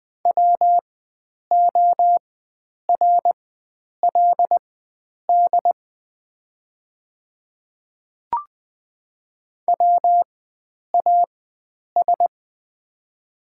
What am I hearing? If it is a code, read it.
Morse code, heard as a keyed tone, about 21 wpm, spaced as for 9 wpm, 690 Hz. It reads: WORLD WAS
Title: 4x Character Spacing - Top 300 Words - 20wpm